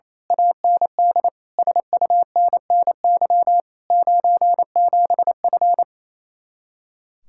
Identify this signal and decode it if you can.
Morse code, heard as a keyed tone, about 26 wpm, 700 Hz. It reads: AND SUNNY 97F
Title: Speed-Racing Common QSO Phrases - 28wpm